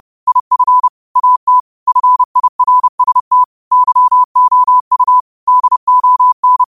Morse code, heard as a keyed tone, about 30 wpm, 1000 Hz. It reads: IF AT FIRST YOU DON
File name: qotdfiles / 30wpm